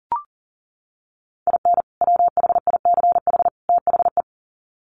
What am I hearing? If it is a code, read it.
Morse code, heard as a keyed tone, about 40 wpm, 700 Hz. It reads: IN WHICH THE
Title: Sets of 3 Words 40wpm